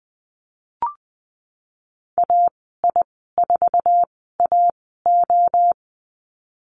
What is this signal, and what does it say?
Morse code, heard as a keyed tone, about 20 wpm, spaced as for 15 wpm, 700 Hz. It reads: AI4AO
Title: 069 Lesson 12g - Letter d - Call Signs 20wpm